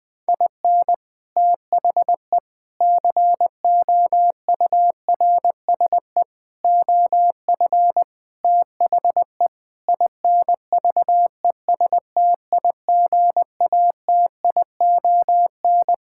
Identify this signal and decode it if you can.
Morse code, heard as a keyed tone, about 20 wpm, 700 Hz. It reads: IN THE COURSE OF THE INVESTIGATION
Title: Sentences with 5-Word Phrases - 20wpm